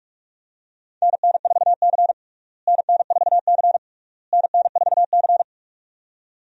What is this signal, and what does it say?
Morse code, heard as a keyed tone, about 46 wpm, 700 Hz. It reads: NN4C NN4C NN4C
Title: USA Call Signs - 4 Characters - Repeated 3x - 45wpm